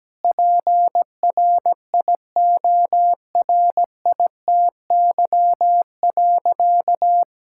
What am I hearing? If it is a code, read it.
Morse code, heard as a keyed tone, about 17 wpm, 700 Hz. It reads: PRIORITY.
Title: Sentences from Top 1600 Words 17wpm